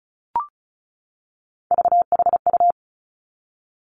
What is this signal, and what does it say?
Morse code, heard as a keyed tone, about 36 wpm, 700 Hz. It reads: VHU
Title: Three Challenging Letter-Number Combinations 35wpm